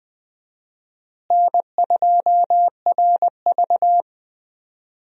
Morse code, heard as a keyed tone, about 20 wpm, 700 Hz. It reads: N2RV
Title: USA Call Signs - 4 Characters - Repeated 3x - 20wpm